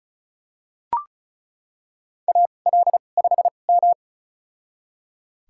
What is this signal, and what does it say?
Morse code, heard as a keyed tone, about 36 wpm, spaced as for 25 wpm, 700 Hz. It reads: AL5M
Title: USA Call Signs - 2x Character Spacing - 35wpm